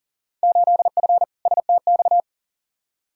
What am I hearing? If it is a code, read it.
Morse code, heard as a keyed tone, about 40 wpm, 700 Hz. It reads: 8F STX
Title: Field Day Exchanges Encore - 40wpm